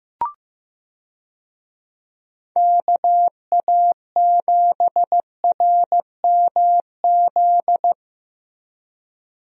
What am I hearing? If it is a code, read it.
Morse code, heard as a keyed tone, about 15 wpm, 700 Hz. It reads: KA7RMZ
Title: USA Call Signs - 6 Characters - Repeated 2x - 15wpm